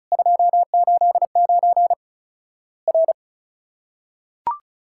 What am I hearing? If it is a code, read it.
Morse code, heard as a keyed tone, about 36 wpm, 700 Hz. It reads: 289 R
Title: Worked All Europe DX Contest - QTC Exchange - 35wpm